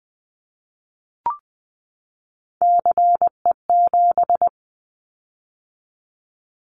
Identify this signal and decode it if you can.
Morse code, heard as a keyed tone, about 20 wpm, 700 Hz. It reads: CE7
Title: Callsign Prefixes for Top 100 DXCC - 20wpm